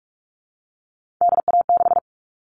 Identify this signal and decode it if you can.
Morse code, heard as a keyed tone, about 44 wpm, 700 Hz. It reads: DA6
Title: All Cut Numbers - 3-Digit 45wpm